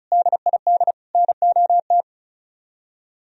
Morse code, heard as a keyed tone, about 44 wpm, 700 Hz. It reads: DID NOT